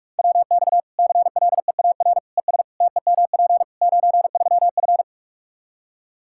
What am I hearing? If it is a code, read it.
Morse code, heard as a keyed tone, about 46 wpm, 700 Hz. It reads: WX CLEAR ES TEMP 93F
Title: K1USN SST QSO Practice 45wpm